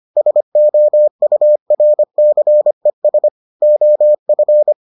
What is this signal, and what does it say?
Morse code, heard as a keyed tone, about 25 wpm, 600 Hz. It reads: SOURCES OF